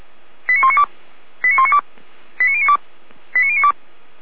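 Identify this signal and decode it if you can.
Five-tone selective call (ZVEI), heard as quick runs of five tones, tones of about 80 ms. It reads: 89192 89192 89092 89092
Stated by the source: ZVEI five-tone selective call received by SDR.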